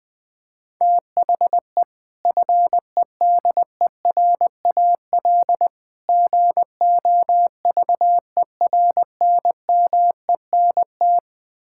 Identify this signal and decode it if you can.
Morse code, heard as a keyed tone, about 20 wpm, 700 Hz. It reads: THE FEDERAL GOVERNMENT
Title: Build A Phrase - 3-Word Phrases - 20wpm